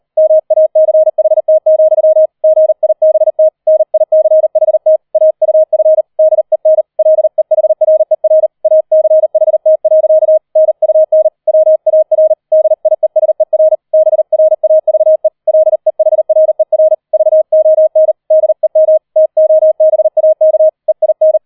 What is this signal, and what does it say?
Morse code, heard as a keyed tone, about 38 wpm, 630 Hz. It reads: MACHT, GIBT NICHT AUF DEN LEHRER ACHT. NUN WAR DIESER BRAVE LEHRER VON DEM TOBAK EIN
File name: 004 Vierter Streich 40wpm